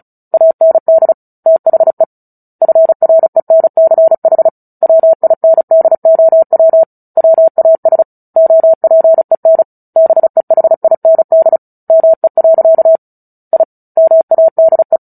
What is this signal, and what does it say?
Morse code, heard as a keyed tone, about 34 wpm, 650 Hz. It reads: AND THE FRENCH WINDOW WAS OPEN BEHIND ME. I MADE
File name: Book Two / 09-Chapter 9 - 35 WPM